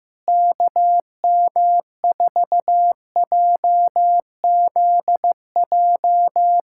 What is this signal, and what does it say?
Morse code, heard as a keyed tone, about 15 wpm, 700 Hz. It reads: KM4JZJ
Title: USA Callsign DE Callsign - 6 Characters - 15wpm